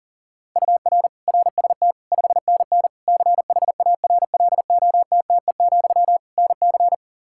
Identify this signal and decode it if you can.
Morse code, heard as a keyed tone, about 37 wpm, 700 Hz. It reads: UR RST 5NN CHARLOTTE, NC
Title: K1USN SST QSO Practice 40wpm